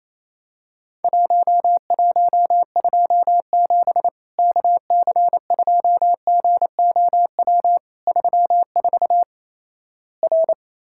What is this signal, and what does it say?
Morse code, heard as a keyed tone, about 28 wpm, 700 Hz. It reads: 1127 KC2GOW 34 R
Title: Worked All Europe DX Contest - QTC Exchange - 28wpm